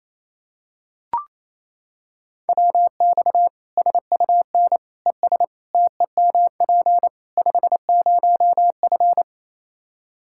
Morse code, heard as a keyed tone, about 28 wpm, 700 Hz. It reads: WX SUN ES TEMP 50F
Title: Practice QSO - Exchange 2 - Info - 28wpm